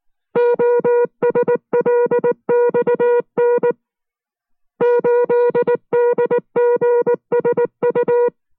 Morse code, heard as a keyed tone, about 19 wpm, 480 Hz. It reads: OSLXN 8DGSU